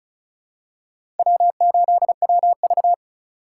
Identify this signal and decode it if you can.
Morse code, heard as a keyed tone, about 34 wpm, 700 Hz. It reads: W8WV
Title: USA Call Signs - 4 Characters - 2x Character Spacing - 35wpm